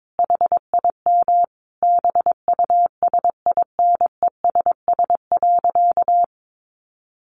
Morse code, heard as a keyed tone, about 22 wpm, 700 Hz. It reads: HIM BUSINESS.
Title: Sentences from Top 600 Words 22wpm